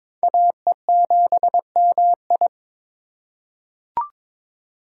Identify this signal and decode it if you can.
Morse code, heard as a keyed tone, about 22 wpm, 700 Hz. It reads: AE7MI
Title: USA Call Signs - 5 Characters - 2x Character Spacing - 22wpm